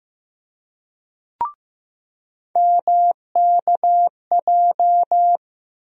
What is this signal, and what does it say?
Morse code, heard as a keyed tone, about 15 wpm, 700 Hz. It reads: MKJ
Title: USA Call Sign Postfixes - 3 Letters 15wpm